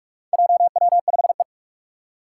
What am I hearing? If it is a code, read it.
Morse code, heard as a keyed tone, about 46 wpm, 700 Hz. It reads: JW5E
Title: Four Challenging Letter-Number Combinations 45wpm